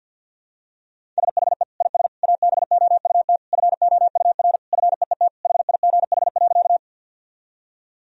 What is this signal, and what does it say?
Morse code, heard as a keyed tone, about 46 wpm, 700 Hz. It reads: SHE IS ABOUT FOUR FEET HIGH.
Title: Sentences from Top 1000 Words - Review Part 1 - 50wpm